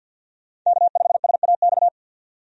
Morse code, heard as a keyed tone, about 50 wpm, 700 Hz. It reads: K5SAX
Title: USA Call Signs - 5 Characters - 50wpm